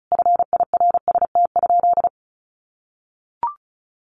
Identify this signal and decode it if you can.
Morse code, heard as a keyed tone, about 34 wpm, 700 Hz. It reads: FIRST?
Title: Sentences from Top 1100 Words 35wpm